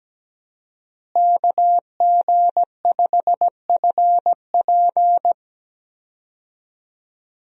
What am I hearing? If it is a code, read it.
Morse code, heard as a keyed tone, about 17 wpm, 700 Hz. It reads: KG5FP
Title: USA Call Signs - 5 Characters - Repeated 3x - 17wpm